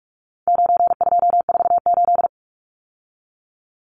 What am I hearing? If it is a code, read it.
Morse code, heard as a keyed tone, about 44 wpm, 700 Hz. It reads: 9248
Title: Numbers - 4-Digit 45wpm